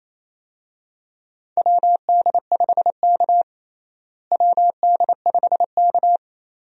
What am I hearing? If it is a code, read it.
Morse code, heard as a keyed tone, about 28 wpm, 700 Hz. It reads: WD5K WD5K
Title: USA Call Signs - 4 Characters - Repeated 2x - 28wpm